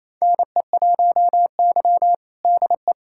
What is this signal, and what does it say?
Morse code, heard as a keyed tone, about 28 wpm, 700 Hz. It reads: NE1Y DE
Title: USA Callsign DE Callsign 28wpm